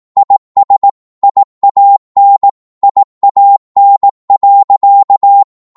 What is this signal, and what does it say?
Morse code, heard as a keyed tone, about 18 wpm, 800 Hz. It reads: IS IAN IAN.
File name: Morse / qso-18wpm